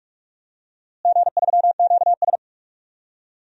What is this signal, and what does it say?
Morse code, heard as a keyed tone, about 46 wpm, 700 Hz. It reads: G3QS
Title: DXCC-100 Call Signs - 4 Characters - Repeated 3x - 45wpm